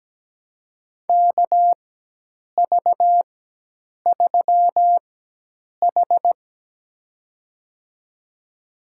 Morse code, heard as a keyed tone, about 17 wpm, spaced as for 8 wpm, 700 Hz. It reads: KV3H
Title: USA Call Signs - 4 Characters - 4x Character Spacing - 17wpm